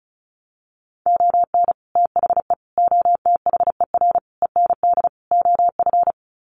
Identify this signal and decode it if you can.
Morse code, heard as a keyed tone, about 34 wpm, 700 Hz. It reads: ON THE OTHER END OF
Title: Sets of 5 Words Encore 35wpm